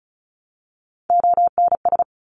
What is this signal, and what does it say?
Morse code, heard as a keyed tone, about 36 wpm, 700 Hz. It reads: ONS
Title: ARRL Sections 35wpm